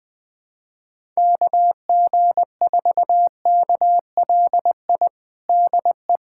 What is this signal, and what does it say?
Morse code, heard as a keyed tone, about 20 wpm, 700 Hz. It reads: KG4KLI DE
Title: USA Callsign DE Callsign 20wpm